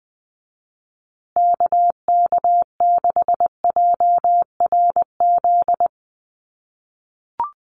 Morse code, heard as a keyed tone, about 20 wpm, 700 Hz. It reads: KK6JRZ
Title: USA Call Signs Repeated 2x 20wpm